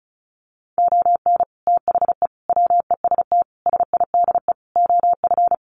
Morse code, heard as a keyed tone, about 34 wpm, 700 Hz. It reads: ON THE WEST SIDE OF
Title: Sets of 5 Words Encore2 35wpm